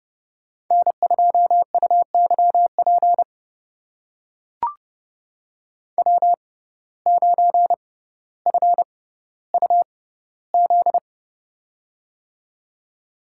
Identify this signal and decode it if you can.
Morse code, heard as a keyed tone, about 31 wpm, spaced as for 16 wpm, 700 Hz. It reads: N2UYP W9FUZ
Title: USA Call Signs - 5 Characters - 6x Character Spacing - 30wpm